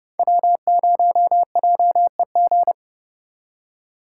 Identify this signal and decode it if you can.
Morse code, heard as a keyed tone, about 36 wpm, 700 Hz. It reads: W0JEG
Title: Speed Racing - USA 1x3 Call Signs 30wpm